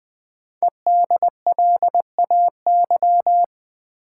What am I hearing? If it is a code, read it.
Morse code, heard as a keyed tone, about 20 wpm, 700 Hz. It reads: EDLAY
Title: Code Groups - 5 Letters - 20wpm